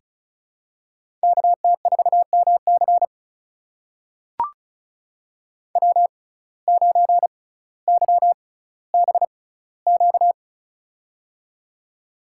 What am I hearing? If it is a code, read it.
Morse code, heard as a keyed tone, about 36 wpm, spaced as for 18 wpm, 700 Hz. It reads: KT4MC W9YBQ
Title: USA Call Signs - 5 Characters - 6x Character Spacing - 35wpm